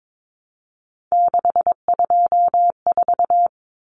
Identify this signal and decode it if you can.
Morse code, heard as a keyed tone, about 22 wpm, 700 Hz. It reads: 624
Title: Numbers - 3-Digit 22wpm